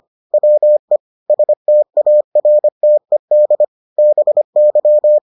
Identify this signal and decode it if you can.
Morse code, heard as a keyed tone, about 25 wpm, 600 Hz. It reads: WE STARTED BY